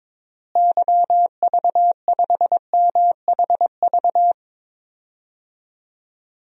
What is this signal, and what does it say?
Morse code, heard as a keyed tone, about 22 wpm, 700 Hz. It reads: YV5MHV
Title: DXCC-100 Call Signs - 6 Characters - Repeated 3x - 22wpm